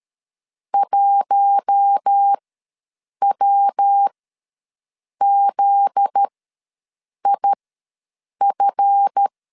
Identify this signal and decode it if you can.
Morse code, heard as a keyed tone, about 13 wpm, spaced as for 7 wpm, 800 Hz. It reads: 1WZIF
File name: LEZIONI / 7 GF9Z / 7 GF9Z ws 12wpm 3321